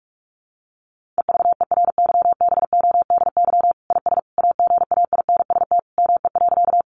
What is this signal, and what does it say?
Morse code, heard as a keyed tone, about 44 wpm, 700 Hz. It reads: EVERYBODY IS AGAINST ME.
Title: Sentences from Top 1400 Words 45wpm